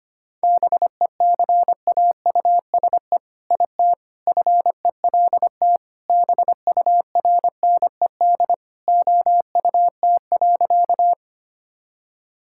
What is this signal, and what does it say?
Morse code, heard as a keyed tone, about 25 wpm, 700 Hz. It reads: BECAUSE IT FELT BURNED OUT.
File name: practice / 25wpm / All-Age Appropriate Jokes - 25wpm